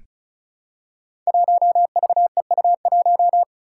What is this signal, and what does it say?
Morse code, heard as a keyed tone, about 36 wpm, 700 Hz. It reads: 1VEU1
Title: Five Challenging Letter-Number Combinations 35wpm